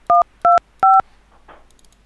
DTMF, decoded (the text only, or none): 125